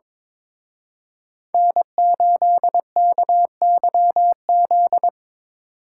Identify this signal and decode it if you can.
Morse code, heard as a keyed tone, about 25 wpm, 700 Hz. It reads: N8KYZ
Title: Speed Racing - USA 1x3 Call Signs 22wpm